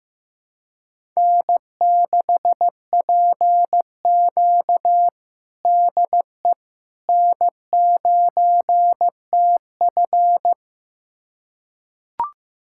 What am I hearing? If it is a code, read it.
Morse code, heard as a keyed tone, about 15 wpm, 700 Hz. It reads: N6PQ DE N9TF
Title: USA Callsign DE Callsign - 4 Characters - 15wpm